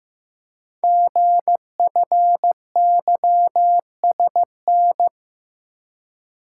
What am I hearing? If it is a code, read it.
Morse code, heard as a keyed tone, about 15 wpm, 700 Hz. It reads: GFYSN
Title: Code Groups - 5 Letter:Numbers - 15wpm